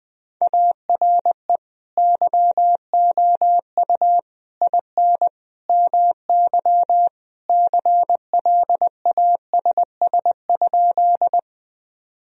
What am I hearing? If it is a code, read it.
Morse code, heard as a keyed tone, about 20 wpm, 700 Hz. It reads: ARE YOU IN MY CLASS?
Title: Sentences from Top 1000 Words - Review Part 2 - 20wpm